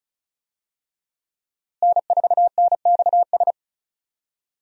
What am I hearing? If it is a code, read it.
Morse code, heard as a keyed tone, about 35 wpm, 700 Hz. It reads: N4NXS
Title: USA Call Signs - 5 Characters - Repeated 2x - 35wpm